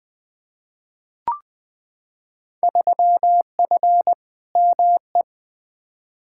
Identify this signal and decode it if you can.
Morse code, heard as a keyed tone, about 20 wpm, 700 Hz. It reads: 3F ME
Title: Field Day Exchanges - 20wpm